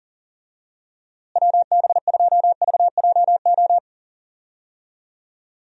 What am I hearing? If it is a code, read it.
Morse code, heard as a keyed tone, about 40 wpm, 700 Hz. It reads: WB2VJO
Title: USA Call Signs - 6 Characters - Repeated 2x - 40wpm